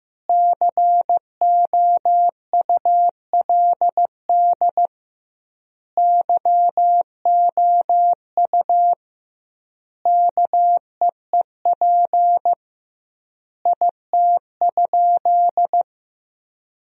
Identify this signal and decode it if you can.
Morse code, heard as a keyed tone, about 15 wpm, 700 Hz. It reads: COULD YOU KEEP IT?
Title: 2x Word-Spacing - Sentences from Top 300 Words - 15wpm